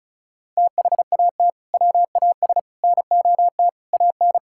Morse code, heard as a keyed tone, about 36 wpm, 690 Hz. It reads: THAT WAS NOT AN
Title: Sentences from Top 1000 Words - Review Part 5 - 35wpm